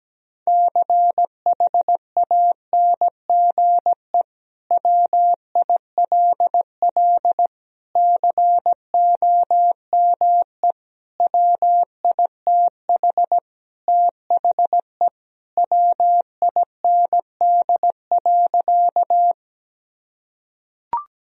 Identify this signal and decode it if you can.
Morse code, heard as a keyed tone, about 17 wpm, 700 Hz. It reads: CHANGE WILL COME WITH THE WIND.
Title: Sentences from Top 1000 Words - Review Part 2 - 17wpm